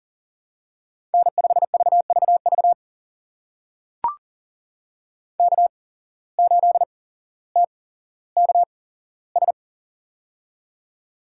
Morse code, heard as a keyed tone, about 40 wpm, spaced as for 17 wpm, 700 Hz. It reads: N5VVV K8TKS
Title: USA Call Signs - 5 Characters - 8x Character Spacing - 40wpm